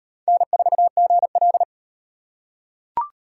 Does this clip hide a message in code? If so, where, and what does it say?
Yes: Morse code from 0.3 to 1.6 s, heard as a keyed tone, about 36 wpm, 700 Hz. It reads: N4GL